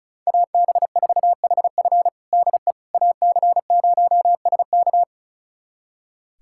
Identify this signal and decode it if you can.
Morse code, heard as a keyed tone, about 34 wpm, 700 Hz. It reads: AB4HF DE AC0SK
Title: USA Callsign DE Callsign - 5 Characters - 35wpm